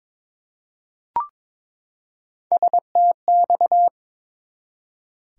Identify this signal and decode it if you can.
Morse code, heard as a keyed tone, about 22 wpm, 700 Hz. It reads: STX
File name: practice / 22wpm / ARRL Sections v2 - 22wpm